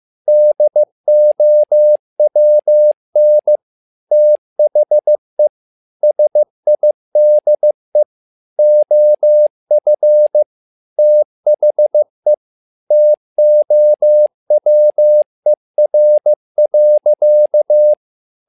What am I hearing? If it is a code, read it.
Morse code, heard as a keyed tone, about 15 wpm, 600 Hz. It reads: DOWN THE SIDE OF THE TOWER.
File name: learning center / tower trouble / tower trouble 15wpm